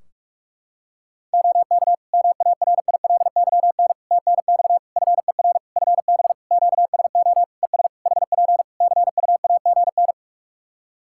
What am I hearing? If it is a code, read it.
Morse code, heard as a keyed tone, about 46 wpm, 700 Hz. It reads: OK MARILYN TNX FER FB QSO ES HP CUAGN
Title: Practice QSO - Exchange 4 - Ending - 45wpm